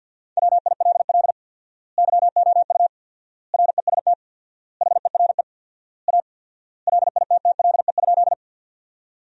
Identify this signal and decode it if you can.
Morse code, heard as a keyed tone, about 50 wpm, 700 Hz. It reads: WILL YOU REST HERE A LITTLE?
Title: 4x Word-Spacing - Sentences from Top 500 Words - 50wpm